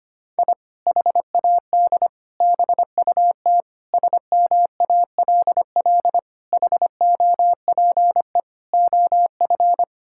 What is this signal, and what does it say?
Morse code, heard as a keyed tone, about 25 wpm, 700 Hz. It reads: I HAD BUT SMALL HOPE OF
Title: Sentences from Top 1000 Words - Review Part 6 - 25wpm